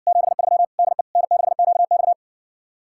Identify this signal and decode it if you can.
Morse code, heard as a keyed tone, about 60 wpm, 700 Hz. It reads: 73 DE N6QX